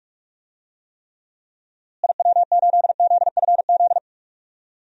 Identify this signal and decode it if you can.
Morse code, heard as a keyed tone, about 46 wpm, 700 Hz. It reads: IW8ZFZ
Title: DXCC-100 Call Signs - 6 Characters - Repeated 3x - 45wpm